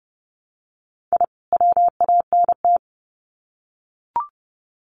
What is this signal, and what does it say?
Morse code, heard as a keyed tone, about 30 wpm, 700 Hz. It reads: I WANT
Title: Sets of 2 Words 30wpm